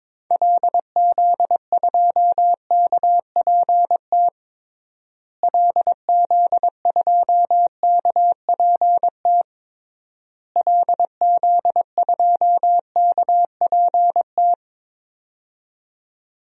Morse code, heard as a keyed tone, about 22 wpm, 700 Hz. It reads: LZ2KPT LZ2KPT LZ2KPT
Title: DXCC-100 Call Signs - 6 Characters - Repeated 3x - 22wpm